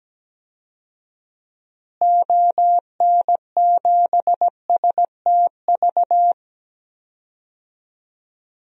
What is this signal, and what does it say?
Morse code, heard as a keyed tone, about 17 wpm, 700 Hz. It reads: ON7STV
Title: DXCC-100 Call Signs - 6 Characters - Repeated 2x - 17wpm